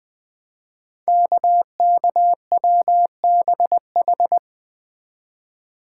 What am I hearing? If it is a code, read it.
Morse code, heard as a keyed tone, about 20 wpm, 700 Hz. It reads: KKWBH
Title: Code Groups - 5 Letters - 20wpm